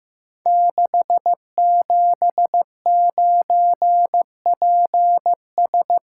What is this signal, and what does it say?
Morse code, heard as a keyed tone, about 15 wpm, 700 Hz. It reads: 679PS
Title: Code Groups - 5 Letter:Numbers - 15wpm